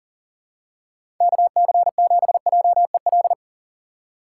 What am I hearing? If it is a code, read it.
Morse code, heard as a keyed tone, about 40 wpm, 700 Hz. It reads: KC7JEL
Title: USA Call Signs - 6 Characters - 2x Character Spacing - 40wpm